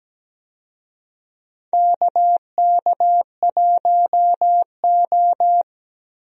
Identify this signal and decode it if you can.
Morse code, heard as a keyed tone, about 17 wpm, 700 Hz. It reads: KK1O
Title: USA Call Signs - 4 Characters - Repeated 3x - 17wpm